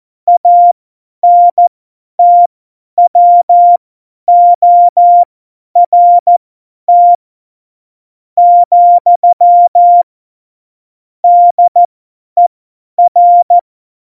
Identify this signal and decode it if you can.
Morse code, heard as a keyed tone, about 14 wpm, spaced as for 10 wpm, 700 Hz. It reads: ANTWORT , DER
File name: Morsekurs / Witze / Witz 23. Oktober 2018, 50 F70